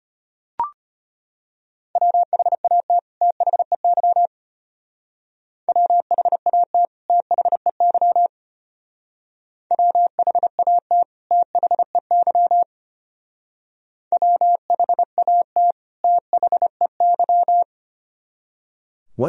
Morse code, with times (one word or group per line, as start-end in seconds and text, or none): 2.0-3.0 WHAT
3.2-4.3 THEY
5.7-6.9 WHAT
7.1-8.3 THEY
9.7-11.0 WHAT
11.3-12.6 THEY
14.1-15.7 WHAT
16.0-17.6 THEY